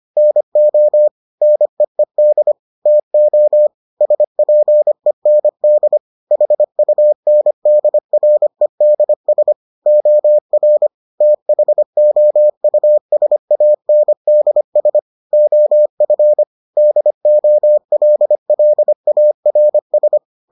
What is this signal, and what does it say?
Morse code, heard as a keyed tone, about 25 wpm, 600 Hz. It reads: NO NEED TO SPEND HUNDREDS OR THOUSANDS OF DOLLARS